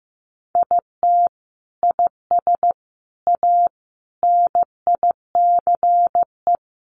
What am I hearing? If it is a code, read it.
Morse code, heard as a keyed tone, about 15 wpm, 700 Hz. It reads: IT IS A NICE
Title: Sentences from Top 1100 Words 15wpm